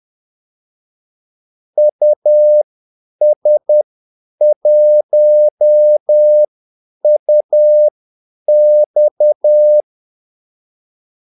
Morse code, heard as a keyed tone, about 10 wpm, spaced as for 8 wpm, 600 Hz. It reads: US1UX
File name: Callsigns-10-8wpm